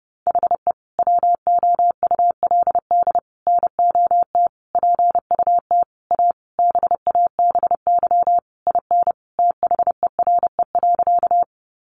Morse code, heard as a keyed tone, about 30 wpm, 700 Hz. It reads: HE WOULD NOT PUT A BABY IN THERE.
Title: Easy Phrases 30wpm